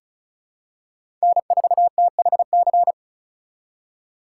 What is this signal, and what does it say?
Morse code, heard as a keyed tone, about 36 wpm, 700 Hz. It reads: N4THC
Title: USA Call Signs - 5 Characters - 4x Character Spacing - 35wpm